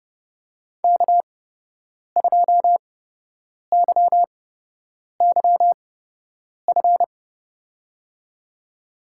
Morse code, heard as a keyed tone, about 30 wpm, spaced as for 8 wpm, 700 Hz. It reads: K2YYF
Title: USA Call Signs - 5 Characters - 8x Character Spacing - 30wpm